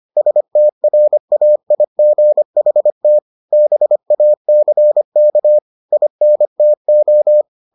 Morse code, heard as a keyed tone, about 25 wpm, 600 Hz. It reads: STRAIGHT BACK INTO